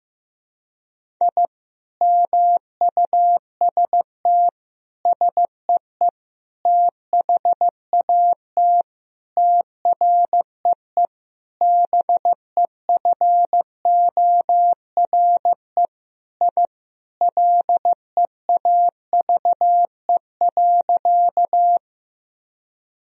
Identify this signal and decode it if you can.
Morse code, heard as a keyed tone, about 15 wpm, 700 Hz. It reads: I MUST SEE THAT TREE BEFORE I LEAVE.
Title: Sentences from Top 1000 Words - Review Part 1 - 15wpm